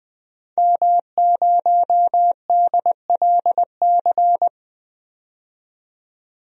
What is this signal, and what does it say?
Morse code, heard as a keyed tone, about 20 wpm, 700 Hz. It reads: M0DLC
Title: DXCC-100 Call Signs - 5 Characters - 20wpm